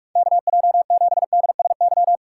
Morse code, heard as a keyed tone, about 46 wpm, 700 Hz. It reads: KJ7DSY
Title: USA Call Signs - 6 Characters - 4x Character Spacing - 45wpm